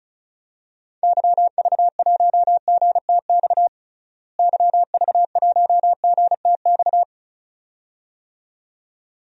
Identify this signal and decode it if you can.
Morse code, heard as a keyed tone, about 34 wpm, 700 Hz. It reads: YV1GTX YV1GTX
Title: DXCC-100 Call Signs - 6 Characters - Repeated 2x - 35wpm